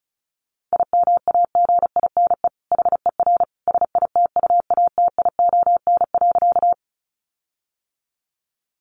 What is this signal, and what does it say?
Morse code, heard as a keyed tone, about 34 wpm, 700 Hz. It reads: IMAGINE HER SITUATION.
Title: Sentences from Top 1400 Words 35wpm